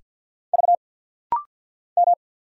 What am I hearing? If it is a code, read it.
Morse code, heard as a keyed tone, about 50 wpm, 690 Hz. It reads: V M
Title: Single Letters - Rapid-Fire 50wpm